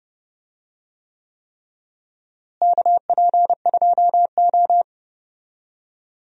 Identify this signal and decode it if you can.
Morse code, heard as a keyed tone, about 32 wpm, 700 Hz. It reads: KP2O
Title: DXCC-100 Call Signs - 4 Characters - Repeated 3x - 30wpm